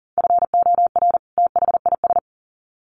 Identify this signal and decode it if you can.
Morse code, heard as a keyed tone, about 40 wpm, 700 Hz. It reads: FOR THIS
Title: Sets of 2 Words 40wpm